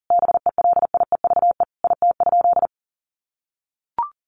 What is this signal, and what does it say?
Morse code, heard as a keyed tone, about 40 wpm, 700 Hz. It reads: BELIEVE IT?
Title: Sentences from Top 1000 Words 40wpm